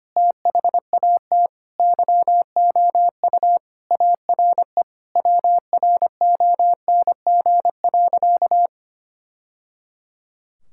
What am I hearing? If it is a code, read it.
Morse code, heard as a keyed tone, about 25 wpm, 700 Hz. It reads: THAT YOU ARE WRONG.
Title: Sentences from Top 1000 Words - Review Part 5 - 25wpm